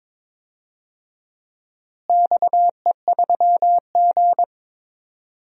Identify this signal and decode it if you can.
Morse code, heard as a keyed tone, about 22 wpm, 700 Hz. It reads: XE3G
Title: DXCC-100 Call Signs - 4 Characters - Repeated 3x - 22wpm